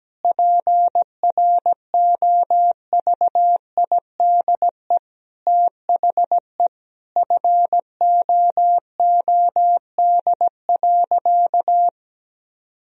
Morse code, heard as a keyed tone, about 17 wpm, 700 Hz. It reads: PROVIDE THE FOOD.
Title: Sentences from Top 1000 Words - Review Part 3 - 17wpm